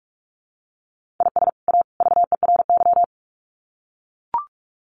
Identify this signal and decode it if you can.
Morse code, heard as a keyed tone, about 44 wpm, 700 Hz. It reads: IS A VERY
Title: Sets of 3 Words 45wpm